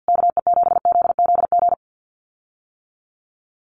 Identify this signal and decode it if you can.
Morse code, heard as a keyed tone, about 50 wpm, 700 Hz. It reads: KE7ZZG
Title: USA Call Signs Repeated 2x 50wpm